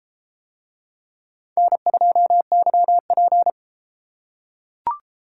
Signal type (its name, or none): Morse code